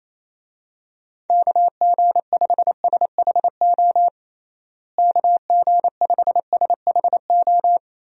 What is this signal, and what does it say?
Morse code, heard as a keyed tone, about 28 wpm, 700 Hz. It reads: KG5SHO KG5SHO
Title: USA Call Signs - 6 Characters - Repeated 2x - 28wpm